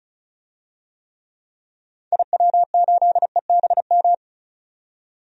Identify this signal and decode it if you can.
Morse code, heard as a keyed tone, about 35 wpm, 700 Hz. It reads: IW8EBM
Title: DXCC-100 Call Signs - 6 Characters - 35wpm